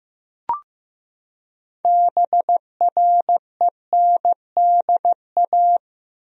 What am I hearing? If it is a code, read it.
Morse code, heard as a keyed tone, about 15 wpm, 700 Hz. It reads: BRENDA
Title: USA Common Names - 15wpm